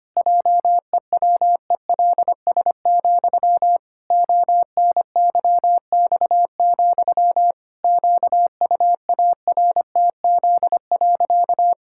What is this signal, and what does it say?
Morse code, heard as a keyed tone, about 25 wpm, 700 Hz. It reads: JEWELS, ONYX, QUARTZ.
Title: Pangram Challenge v2 - 25wpm